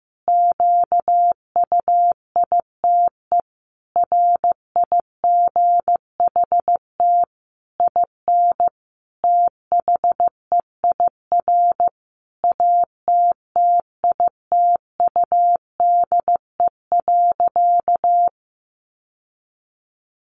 Morse code, heard as a keyed tone, about 14 wpm, 700 Hz. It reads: QUITE RIGHT IN THEIR ATTITUDE.
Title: Sentences from Top 1300 Words 15wpm